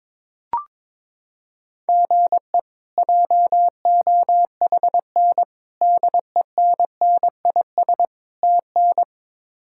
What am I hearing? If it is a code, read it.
Morse code, heard as a keyed tone, about 22 wpm, 700 Hz. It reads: GE JOHN DENNIS TN
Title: K1USN SST QSO Practice - 22wpm